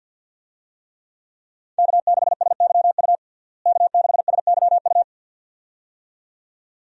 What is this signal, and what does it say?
Morse code, heard as a keyed tone, about 47 wpm, 700 Hz. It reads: K6SYU K6SYU
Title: USA Call Signs - 5 Characters - Repeated 2x - 50wpm